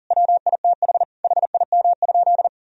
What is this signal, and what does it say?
Morse code, heard as a keyed tone, about 40 wpm, 700 Hz. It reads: WITH HIM?
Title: Sentences from Top 1000 Words - Review Part 3 - 40wpm